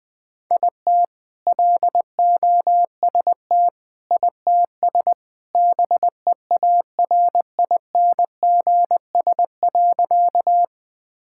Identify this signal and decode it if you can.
Morse code, heard as a keyed tone, about 20 wpm, 700 Hz. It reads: IT LOST ITS BEARINGS.
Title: All-Age Appropriate Jokes - 20wpm